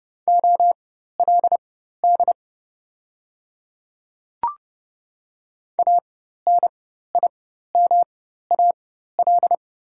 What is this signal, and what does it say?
Morse code, heard as a keyed tone, about 30 wpm, spaced as for 14 wpm, 700 Hz. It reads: OLD ANIMAL
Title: 4x Character Spacing - Top 200 Words - 30wpm